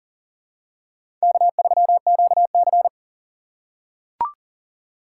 Morse code, heard as a keyed tone, about 40 wpm, 700 Hz. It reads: K3QC
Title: USA Call Signs - 4 Characters - 4x Character Spacing - 40wpm